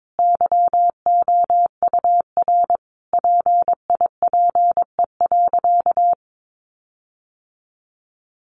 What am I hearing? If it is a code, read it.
Morse code, heard as a keyed tone, about 22 wpm, 700 Hz. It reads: YOUR PIPE.
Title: Sentences from Top 1500 Words 22wpm